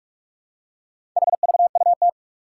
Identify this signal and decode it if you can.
Morse code, heard as a keyed tone, about 44 wpm, 700 Hz. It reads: HVUT
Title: Four Challenging Letter-Number Combinations 45wpm